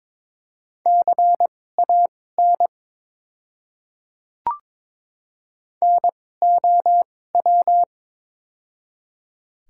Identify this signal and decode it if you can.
Morse code, heard as a keyed tone, about 22 wpm, spaced as for 16 wpm, 690 Hz. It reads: CAN NOW
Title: 2x Character Spacing - Top 100 Words - 22wpm